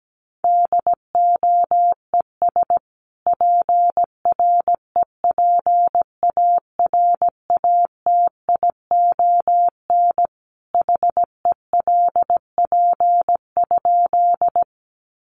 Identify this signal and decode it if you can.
Morse code, heard as a keyed tone, about 17 wpm, 700 Hz. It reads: DOES PREPARATION HELP?
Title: Sentences from Top 1400 Words 17wpm